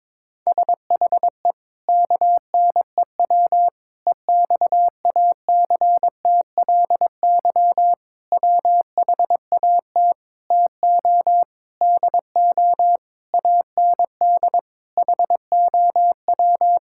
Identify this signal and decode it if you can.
Morse code, heard as a keyed tone, about 22 wpm, 700 Hz. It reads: SHE KNEW EXACTLY WHAT TO DO AND HOW
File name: practice / 22wpm / Sentences with 8-Word Phrases - 22wpm